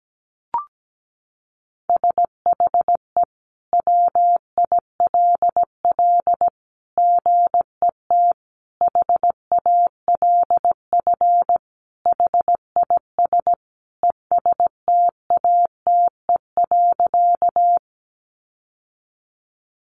Morse code, heard as a keyed tone, about 17 wpm, 700 Hz. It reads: SHE WILL GET HALF HIS ESTATE.
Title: Sentences from Top 1200 Words 17wpm